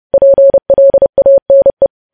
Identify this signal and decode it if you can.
Morse code, heard as a keyed tone, about 30 wpm, 550 Hz. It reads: PLANE
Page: news headlines in morse code at 30 wpm